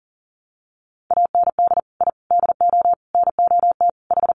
Morse code, heard as a keyed tone, about 40 wpm, 700 Hz. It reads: AND I DO NOT H
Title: Sets of 5 Words Encore 40wpm